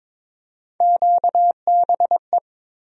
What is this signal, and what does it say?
Morse code, heard as a keyed tone, about 22 wpm, 700 Hz. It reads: QBE
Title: USA Call Sign Postfixes - 3 Letters 22wpm